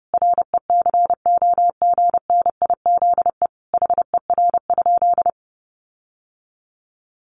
Morse code, heard as a keyed tone, about 30 wpm, 700 Hz. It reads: RECOGNIZE HER?
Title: Sentences from Top 1400 Words 30wpm